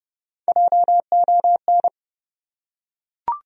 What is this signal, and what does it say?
Morse code, heard as a keyed tone, about 30 wpm, 700 Hz. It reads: JON
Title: USA Common Names - Max 3 Letters - 30wpm